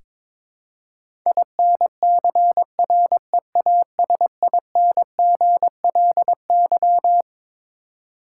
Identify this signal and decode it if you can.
Morse code, heard as a keyed tone, about 21 wpm, 700 Hz. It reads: INCREASINGLY
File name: Twelve Letter Words - 22wpm